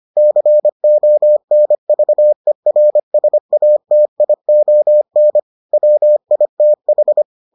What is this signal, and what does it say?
Morse code, heard as a keyed tone, about 25 wpm, 600 Hz. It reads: CONVERSATION WITH